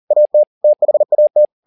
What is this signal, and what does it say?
Morse code, heard as a keyed tone, about 40 wpm, 600 Hz. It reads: AT THAT
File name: the old man and the sea 2 40wpm